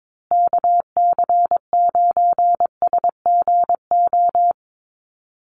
Morse code, heard as a keyed tone, about 22 wpm, 700 Hz. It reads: KC9SGO
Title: USA Call Signs Repeated 3x 22wpm